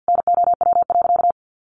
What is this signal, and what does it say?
Morse code, heard as a keyed tone, about 50 wpm, 700 Hz. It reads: NOW.
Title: Sentences from Top 1100 Words 50wpm